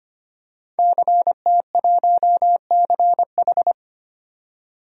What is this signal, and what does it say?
Morse code, heard as a keyed tone, about 25 wpm, 700 Hz. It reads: CT1CH
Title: DXCC-100 Call Signs - 5 Characters - 25wpm